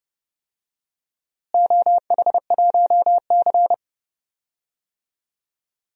Morse code, heard as a keyed tone, about 32 wpm, 700 Hz. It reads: OH1C
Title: DXCC-100 Call Signs - 4 Characters - Repeated 3x - 30wpm